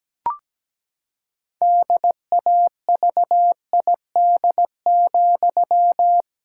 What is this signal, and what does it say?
Morse code, heard as a keyed tone, about 17 wpm, 700 Hz. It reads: DAVID,
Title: Pangram Challenge v2 - 17wpm